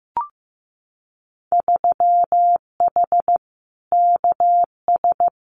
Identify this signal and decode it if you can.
Morse code, heard as a keyed tone, about 15 wpm, 700 Hz. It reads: 3H KS
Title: Winter Field Day Exchanges 15wpm